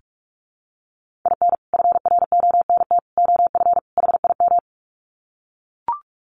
Morse code, heard as a keyed tone, about 44 wpm, 700 Hz. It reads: IN FRONT OF HIM
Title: Sets of 4 Words 45wpm